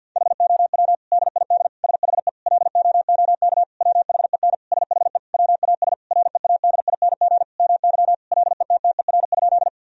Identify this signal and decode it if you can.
Morse code, heard as a keyed tone, about 50 wpm, 700 Hz. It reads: HOW DID SHE LOOK WHEN SHE WAS READING MY LETTER?
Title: Sentences from Top 1000 Words - Review Part 3 - 50wpm